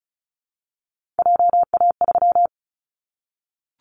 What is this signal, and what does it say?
Morse code, heard as a keyed tone, about 36 wpm, 700 Hz. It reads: JA3
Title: Callsign Prefixes for Top 50 DXCC - 35wpm